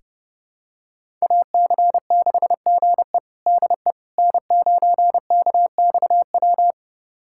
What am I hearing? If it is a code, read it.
Morse code, heard as a keyed tone, about 32 wpm, 700 Hz. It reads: AC6GE DE N9KXW
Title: USA Callsign DE Callsign - 5 Characters - 30wpm